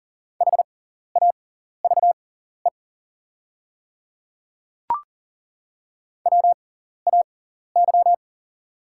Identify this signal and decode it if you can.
Morse code, heard as a keyed tone, about 40 wpm, spaced as for 14 wpm, 700 Hz. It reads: HAVE WAY
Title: 6x Character Spacing - Top 100 Words - 40wpm